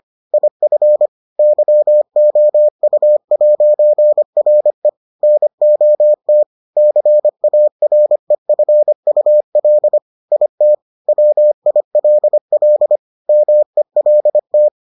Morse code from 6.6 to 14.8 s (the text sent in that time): CAREFUL IT WILL MELT